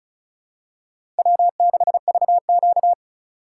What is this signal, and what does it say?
Morse code, heard as a keyed tone, about 36 wpm, 700 Hz. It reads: W6VQ
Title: USA Call Signs - 4x Character Spacing - 35wpm